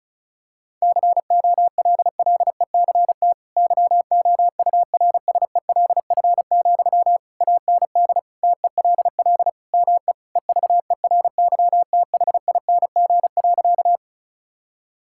Morse code, heard as a keyed tone, about 34 wpm, 700 Hz. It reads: COLLECT YOURSELF, AND TELL ME EVERYTHING.
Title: Sentences from Top 1600 Words 35wpm